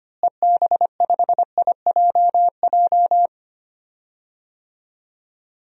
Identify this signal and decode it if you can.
Morse code, heard as a keyed tone, about 25 wpm, 700 Hz. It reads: EB5IJJ
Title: DXCC-100 Call Signs - 6 Characters - Repeated 2x - 25wpm